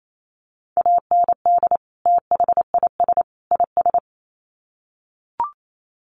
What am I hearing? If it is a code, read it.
Morse code, heard as a keyed tone, about 28 wpm, 700 Hz. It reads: AND THIS IS
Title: Sets of 3 Words 28wpm